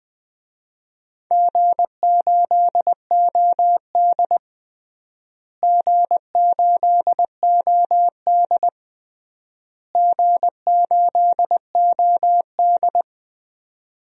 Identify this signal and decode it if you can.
Morse code, heard as a keyed tone, about 20 wpm, 700 Hz. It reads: G8OD G8OD G8OD
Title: DXCC-100 Call Signs - 4 Characters - Repeated 3x - 20wpm